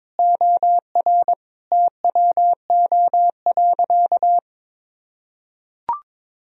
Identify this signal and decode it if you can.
Morse code, heard as a keyed tone, about 22 wpm, 700 Hz. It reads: OR TWO.
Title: Sentences from Top 1000 Words - Review Part 2 - 22wpm